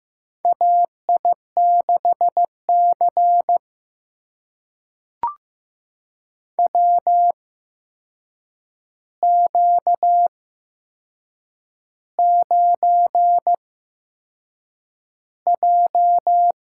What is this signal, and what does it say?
Morse code, heard as a keyed tone, about 15 wpm, spaced as for 6 wpm, 700 Hz. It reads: AI6C WQ9J
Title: USA Call Signs - 4 Characters - 8x Character Spacing - 15wpm